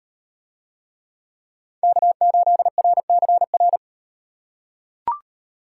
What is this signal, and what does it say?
Morse code, heard as a keyed tone, about 37 wpm, 700 Hz. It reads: K8RCR